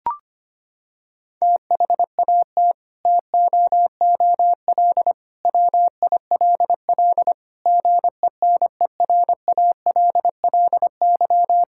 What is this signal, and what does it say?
Morse code, heard as a keyed tone, about 25 wpm, 700 Hz. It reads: THAT TOOL WILL GENERALLY
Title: Sentences from Top 1000 Words - Review Part 6 - 25wpm